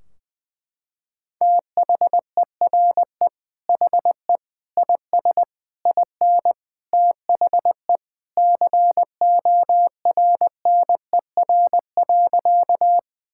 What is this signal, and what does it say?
Morse code, heard as a keyed tone, about 20 wpm, 700 Hz. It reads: THERE HE IS IN THE CORNER.
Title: Sentences from Top 1600 Words 20wpm